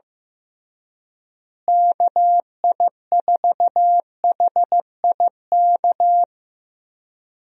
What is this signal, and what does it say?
Morse code, heard as a keyed tone, about 15 wpm, 700 Hz. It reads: KI4HIK
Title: USA Call Signs - 6 Characters - Repeated 3x - 15wpm